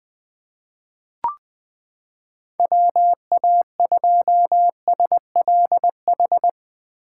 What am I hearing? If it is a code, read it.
Morse code, heard as a keyed tone, about 20 wpm, 700 Hz. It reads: WA2SLH
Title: 2023 POTA Hunter Call Signs - 20wpm